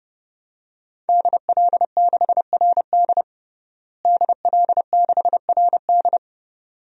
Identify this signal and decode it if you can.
Morse code, heard as a keyed tone, about 30 wpm, 700 Hz. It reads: DL6RD DL6RD
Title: DXCC-100 Call Signs - 5 Characters - Repeated 2x - 30wpm